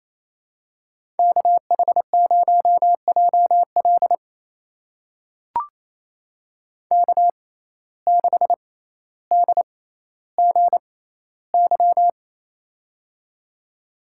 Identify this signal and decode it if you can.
Morse code, heard as a keyed tone, about 28 wpm, spaced as for 14 wpm, 700 Hz. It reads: KH0JL K6DGY
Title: USA Call Signs - 5 Characters - 6x Character Spacing - 28wpm